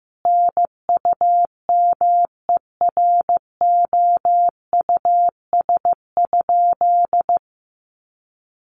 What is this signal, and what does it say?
Morse code, heard as a keyed tone, about 15 wpm, 700 Hz. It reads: NUMEROUS?
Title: Sentences from Top 1400 Words 15wpm